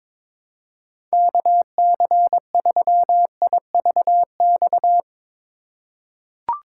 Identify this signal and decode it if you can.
Morse code, heard as a keyed tone, about 22 wpm, 700 Hz. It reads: KC3IVX
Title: USA Call Signs - 6 Characters - 6x Character Spacing - 22wpm